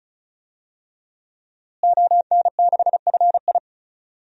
Morse code, heard as a keyed tone, about 36 wpm, 700 Hz. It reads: ON6FI
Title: DXCC-100 Call Signs - 5 Characters - Repeated 3x - 35wpm